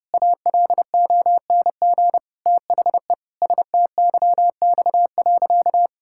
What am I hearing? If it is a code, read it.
Morse code, heard as a keyed tone, about 32 wpm, 700 Hz. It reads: ALONG THE STYX.
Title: Pangram Challenge v2 - 30wpm